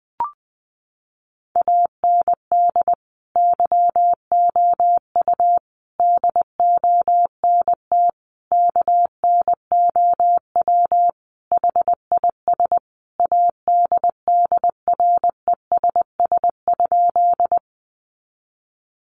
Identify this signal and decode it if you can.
Morse code, heard as a keyed tone, about 20 wpm, 700 Hz. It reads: AND YOU DONT KNOW HIS ADDRESS?
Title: Sentences from Top 1000 Words 20wpm